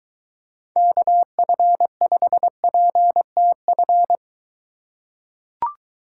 Morse code, heard as a keyed tone, about 23 wpm, 700 Hz. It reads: KF5PTF